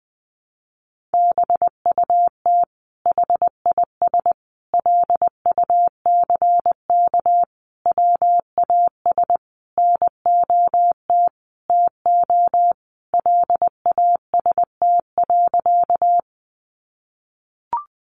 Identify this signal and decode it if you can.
Morse code, heard as a keyed tone, about 20 wpm, 700 Hz. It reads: BUT HIS LUCK WAS NOT TO LAST.
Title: Sentences from Top 1500 Words 20wpm